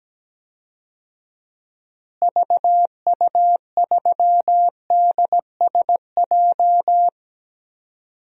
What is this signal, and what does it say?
Morse code, heard as a keyed tone, about 17 wpm, 700 Hz. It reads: VU3DSJ
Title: DXCC-100 Call Signs - 6 Characters - Repeated 3x - 17wpm